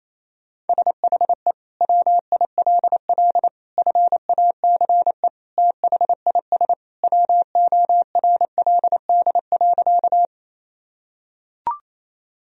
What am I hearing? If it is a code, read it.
Morse code, heard as a keyed tone, about 28 wpm, 700 Hz. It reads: SHE WILL FACE THIS WORLD.
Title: Sentences from Top 1000 Words - Review Part 2 - 28wpm